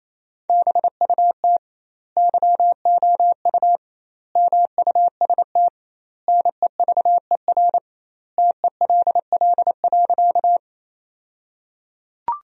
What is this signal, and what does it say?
Morse code, heard as a keyed tone, about 28 wpm, 700 Hz. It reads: BUT YOU MUST NEVER TELL.
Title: 2x Word-Spacing - Sentences from Top 300 Words - 28wpm